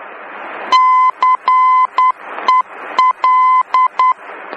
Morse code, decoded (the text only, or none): CEL